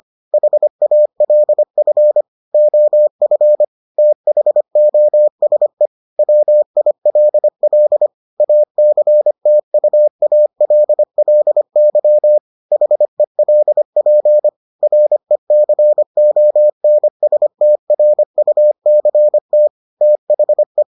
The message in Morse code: HALF OF THOSE WILL ACTUALLY HELP RECONSTRUCT THE

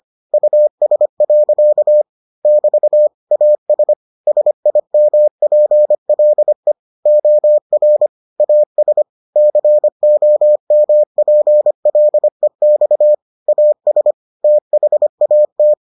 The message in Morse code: US. <BT> AS SIMPLE OR AS COMPLEX AS THAT

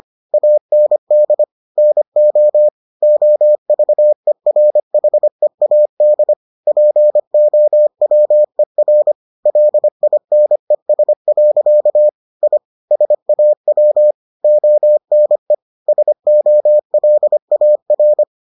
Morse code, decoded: AND NO OVERHEAD POWER LINES. I SAW ONE SOLAR

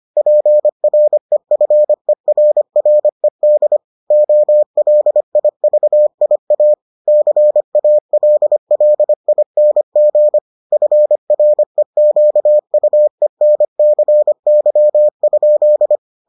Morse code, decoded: PREFERRED OLIVIA CALLING FREQUENCY?

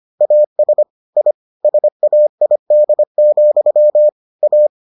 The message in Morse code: AS I SAID, A

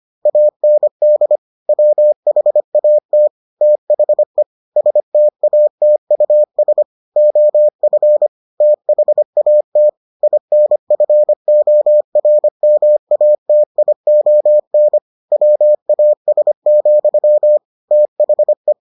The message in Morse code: AND WHAT THE STATUS OF THAT INFORMATION WAS, THE

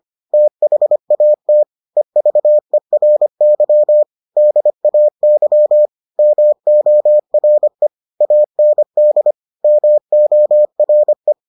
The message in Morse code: THAT EVERY DAY MORE AND MORE

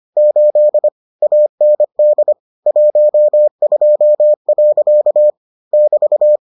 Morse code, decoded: 8 AND 12. <BT>

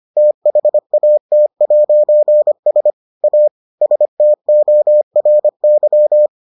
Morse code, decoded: THAT'S A STORY

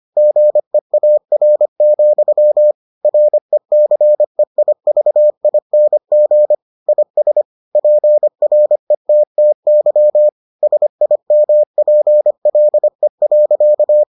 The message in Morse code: GEAR, RECEIVING IS PRETTY SIMPLE.

